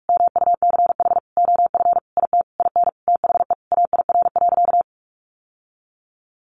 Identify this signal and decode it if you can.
Morse code, heard as a keyed tone, about 44 wpm, 700 Hz. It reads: MUCH OF IT IN THE AIR.
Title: Sentences from Top 900 Words 45wpm